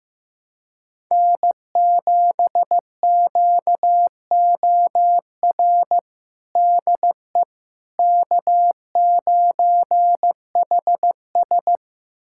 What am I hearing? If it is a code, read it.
Morse code, heard as a keyed tone, about 15 wpm, 700 Hz. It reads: N7QOR DE K9HS
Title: USA Callsign DE Callsign 15wpm